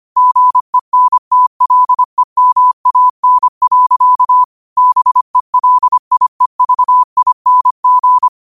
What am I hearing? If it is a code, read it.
Morse code, heard as a keyed tone, about 25 wpm, 1000 Hz. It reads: GENTLEMAN. BELIEVING
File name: 25wpm